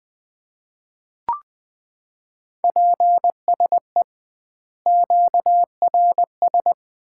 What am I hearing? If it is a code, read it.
Morse code, heard as a keyed tone, about 20 wpm, 700 Hz. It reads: PSE QRS
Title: Common QSO Phrases - 2x Word-Spacing - 20wpm